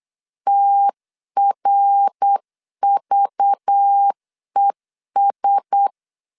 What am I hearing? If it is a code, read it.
Morse code, heard as a keyed tone, about 8 wpm, 800 Hz. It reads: TRVES
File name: CW / DET / 1 6 1500C S C 8wpm 5219